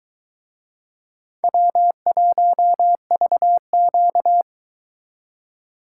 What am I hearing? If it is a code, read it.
Morse code, heard as a keyed tone, about 23 wpm, 700 Hz. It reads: W1VQ